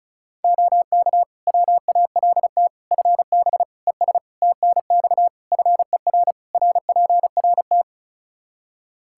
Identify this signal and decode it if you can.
Morse code, heard as a keyed tone, about 34 wpm, 700 Hz. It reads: OK WALT FB ES TNX FER RPRT
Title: Practice QSO - Exchange 2 - Info - 35wpm